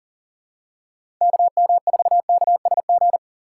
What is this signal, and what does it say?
Morse code, heard as a keyed tone, about 40 wpm, 700 Hz. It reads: KM4KSG
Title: USA Call Signs - 6 Characters - 4x Character Spacing - 40wpm